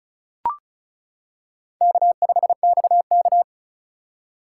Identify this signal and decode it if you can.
Morse code, heard as a keyed tone, about 36 wpm, 700 Hz. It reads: K5XK
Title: K1USN SST QSO Practice - 35wpm